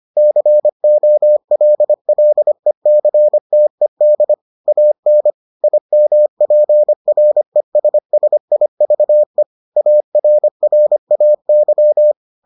Morse code, heard as a keyed tone, about 25 wpm, 600 Hz. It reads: COLLECTED AN IMPRESSIVE ARRAY